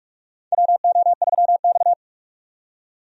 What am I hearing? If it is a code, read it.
Morse code, heard as a keyed tone, about 46 wpm, 700 Hz. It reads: WO3X
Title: USA Call Signs - 4 Characters - 45wpm